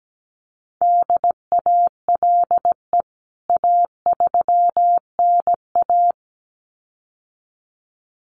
Morse code, heard as a keyed tone, about 17 wpm, 700 Hz. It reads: DALE A3NA
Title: CWT Exchange 17wpm